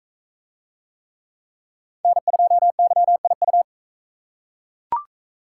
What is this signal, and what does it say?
Morse code, heard as a keyed tone, about 41 wpm, 700 Hz. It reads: N2YIU